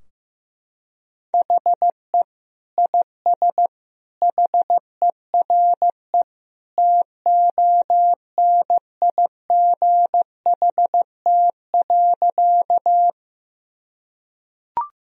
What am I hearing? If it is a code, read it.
Morse code, heard as a keyed tone, about 15 wpm, 700 Hz. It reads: HE IS HERE TONIGHT.
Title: Sentences from Top 1000 Words - Review Part 3 - 15wpm